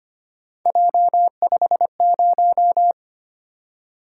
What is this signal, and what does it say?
Morse code, heard as a keyed tone, about 25 wpm, 700 Hz. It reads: J50
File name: Three Letter-Number Combinations - 25wpm